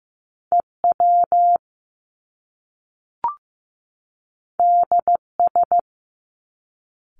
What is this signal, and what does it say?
Morse code, heard as a keyed tone, about 15 wpm, 700 Hz. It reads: EW DS
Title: Top Two Letter Combinations in Words 15wpm